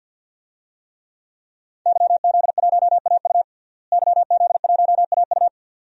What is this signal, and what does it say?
Morse code, heard as a keyed tone, about 48 wpm, 700 Hz. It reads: YZ1AU YZ1AU
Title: DXCC-100 Call Signs - 5 Characters - Repeated 2x - 50wpm